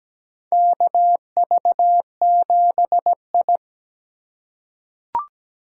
Morse code, heard as a keyed tone, about 17 wpm, 700 Hz. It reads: KV7I
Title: USA Call Signs - 4 Characters - 4x Character Spacing - 17wpm